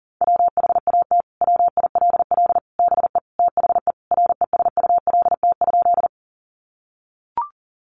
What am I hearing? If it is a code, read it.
Morse code, heard as a keyed tone, about 37 wpm, 700 Hz. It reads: WHAT WILL BE THE RESULT?
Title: Sentences from Top 700 Words 40wpm